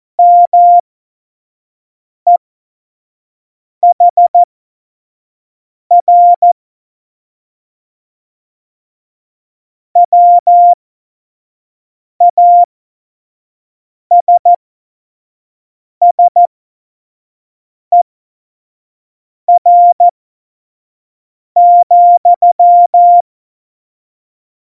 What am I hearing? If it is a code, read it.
Morse code, heard as a keyed tone, about 14 wpm, spaced as for 5 wpm, 700 Hz. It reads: MEHR WASSER,
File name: Morsekurs / Witze / Witz 01. Dezember, 2020 25 F70